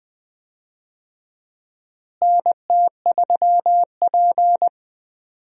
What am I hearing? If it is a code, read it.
Morse code, heard as a keyed tone, about 20 wpm, 700 Hz. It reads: NT3P
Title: USA Call Signs - 4 Characters - Repeated 3x - 20wpm